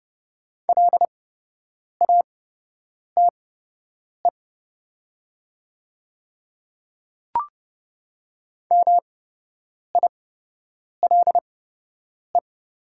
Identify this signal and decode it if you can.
Morse code, heard as a keyed tone, about 30 wpm, spaced as for 8 wpm, 700 Hz. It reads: LATE MILE
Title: 8x Character Spacing - Top 500 Words - 30wpm